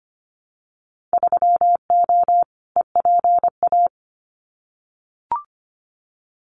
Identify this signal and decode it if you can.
Morse code, heard as a keyed tone, about 25 wpm, 700 Hz. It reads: 3O EPA
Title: Winter Field Day Exchanges 25wpm